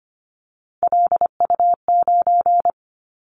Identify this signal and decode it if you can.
Morse code, heard as a keyed tone, about 25 wpm, 700 Hz. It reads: LU9
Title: Callsign Prefixes for Top 100 DXCC - 25wpm